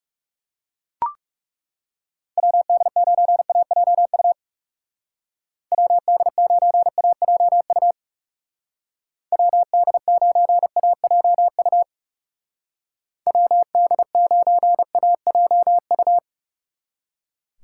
Morse code, heard as a keyed tone, about 37 wpm, 700 Hz. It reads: WD9AJU WD9AJU WD9AJU WD9AJU
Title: Speed Racing - USA 2x3 Call Signs 30wpm